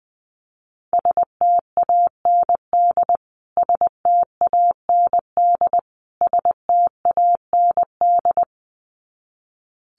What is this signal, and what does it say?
Morse code, heard as a keyed tone, about 20 wpm, 700 Hz. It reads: STAND STAND STAND
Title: Top 500 Words Repeated 20wpm